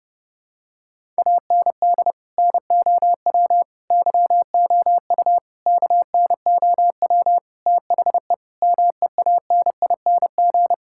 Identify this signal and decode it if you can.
Morse code, heard as a keyed tone, about 32 wpm, 700 Hz. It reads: AND NOW YOU KNOW THE MEANING
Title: Sentences from Top 1000 Words - Review Part 6 - 30wpm